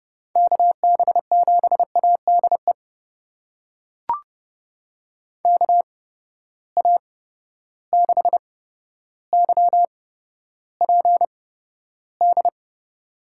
Morse code, heard as a keyed tone, about 30 wpm, spaced as for 13 wpm, 700 Hz. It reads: KB7ADE KA6YPD
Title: USA Call Signs - 6 Characters - 8x Character Spacing - 30wpm